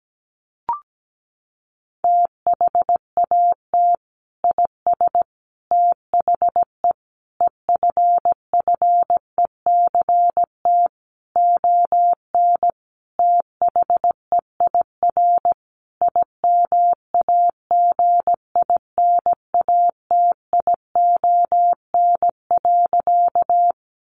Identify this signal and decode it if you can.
Morse code, heard as a keyed tone, about 17 wpm, 700 Hz. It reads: THAT IS THE EFFECT ON THEIR IMAGINATION.
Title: Sentences from Top 1300 Words 17wpm